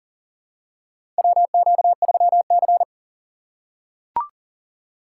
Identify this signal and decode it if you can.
Morse code, heard as a keyed tone, about 40 wpm, 700 Hz. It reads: WQ3C
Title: USA Call Signs - 4 Characters - 3x Character Spacing - 40wpm